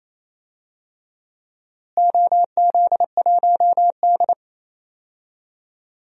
Morse code, heard as a keyed tone, about 28 wpm, 700 Hz. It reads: OZ1D
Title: DXCC-100 Call Signs - 4 Characters - Repeated 2x - 28wpm